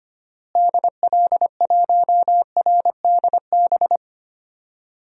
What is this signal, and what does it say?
Morse code, heard as a keyed tone, about 25 wpm, 700 Hz. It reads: DL1RDB
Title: DXCC-100 Call Signs - 6 Characters - Repeated 3x - 25wpm